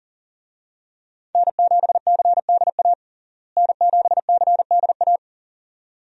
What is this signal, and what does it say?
Morse code, heard as a keyed tone, about 40 wpm, 700 Hz. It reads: N7CDA N7CDA
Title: USA Call Signs - 5 Characters - Repeated 2x - 40wpm